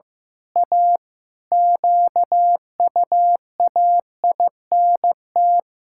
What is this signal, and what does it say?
Morse code, heard as a keyed tone, about 15 wpm, 700 Hz. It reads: A QUAINT
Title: Pangram Challenge v2 - 15wpm